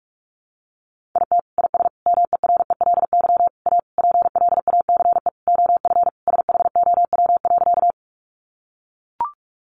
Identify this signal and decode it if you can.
Morse code, heard as a keyed tone, about 44 wpm, 700 Hz. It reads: IT IS MERELY A PLACE OF SHOW.
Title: Sentences from Top 1400 Words 45wpm